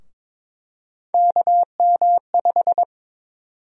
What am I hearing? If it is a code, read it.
Morse code, heard as a keyed tone, about 22 wpm, 700 Hz. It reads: KM5
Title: USA Call Sign Prefixes - Repeated 2x - 22wpm